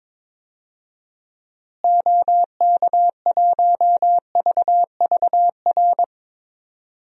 Morse code, heard as a keyed tone, about 22 wpm, 700 Hz. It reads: OK1VVR
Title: DXCC-100 Call Signs - 6 Characters - Repeated 3x - 22wpm